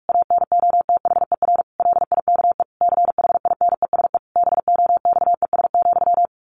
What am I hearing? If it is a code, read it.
Morse code, heard as a keyed tone, about 46 wpm, 700 Hz. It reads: ANOTHER LIKE CHINESE BOXES,
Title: A Sound Of Thunder 45wpm